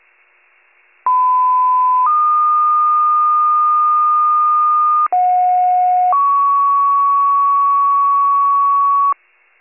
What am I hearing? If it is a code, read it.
Two-tone page, heard as a one-second tone, then a three-second tone, about 1010 then 1220 Hz, about 730 then 1120 Hz. It is heard from 1.1 to 9.1 s.